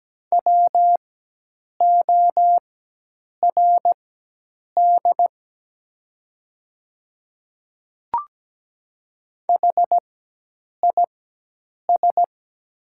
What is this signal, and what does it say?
Morse code, heard as a keyed tone, about 17 wpm, spaced as for 8 wpm, 700 Hz. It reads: WORD HIS
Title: 4x Character Spacing - Top 100 Words - 17wpm